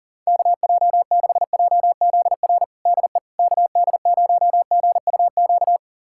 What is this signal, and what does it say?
Morse code, heard as a keyed tone, about 40 wpm, 700 Hz. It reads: KJ6JZR DE KD0GUQ
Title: USA Callsign DE Callsign 40wpm